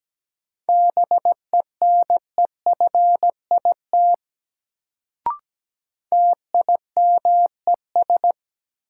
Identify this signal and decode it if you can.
Morse code, heard as a keyed tone, about 17 wpm, 700 Hz. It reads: BENEFIT TIMES
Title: Sentences from Top 1000 Words - Review Part 6 - New Word Drill - 17wpm